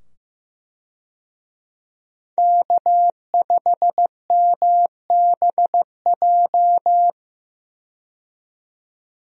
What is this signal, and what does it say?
Morse code, heard as a keyed tone, about 15 wpm, 700 Hz. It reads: K5MBJ
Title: USA Call Signs - 5 Characters - 15wpm